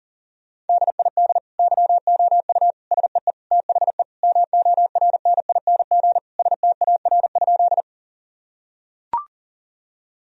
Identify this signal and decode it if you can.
Morse code, heard as a keyed tone, about 40 wpm, 700 Hz. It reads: DID YOU SEE THE MORNING STAR?
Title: Sentences from Top 1000 Words - Review Part 2 - 40wpm